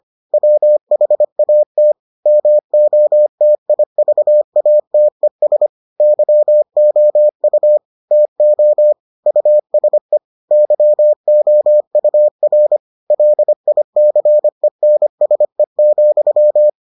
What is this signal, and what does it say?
Morse code, heard as a keyed tone, about 25 wpm, 600 Hz. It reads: WHAT MOTIVATES YOU TO USE YOUR LICENSE,